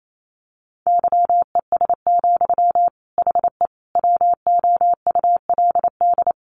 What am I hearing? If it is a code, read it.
Morse code, heard as a keyed tone, about 28 wpm, 700 Hz. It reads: YES, HE WOULD
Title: Sentences from Top 900 Words 28wpm